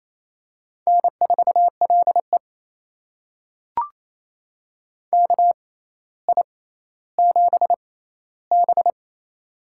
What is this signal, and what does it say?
Morse code, heard as a keyed tone, about 28 wpm, spaced as for 14 wpm, 700 Hz. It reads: N4LE KI7B
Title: USA Call Signs - 4 Characters - 6x Character Spacing - 28wpm